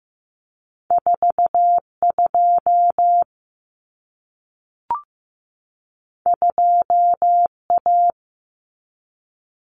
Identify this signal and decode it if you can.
Morse code, heard as a keyed tone, about 15 wpm, 700 Hz. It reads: 42 2A
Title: Common Cut Numbers - 2-Digit 15wpm